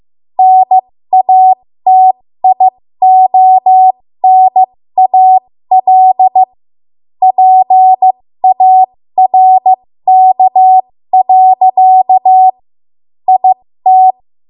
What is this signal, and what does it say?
Morse code, heard as a keyed tone, about 15 wpm, spaced as for 13 wpm, 750 Hz. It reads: NATIONAL PARK. IT
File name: Archive / 13 WPM / 13WPM